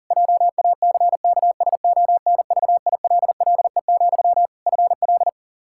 Morse code, heard as a keyed tone, about 40 wpm, 700 Hz. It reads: JACKSONVILLE, FL
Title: Practice QSO - Exchange 1 - Beginning - 40wpm